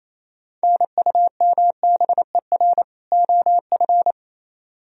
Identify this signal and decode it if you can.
Morse code, heard as a keyed tone, about 34 wpm, 700 Hz. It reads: NUMBER OF
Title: Speed-Racing - Sets of 2 Words 28wpm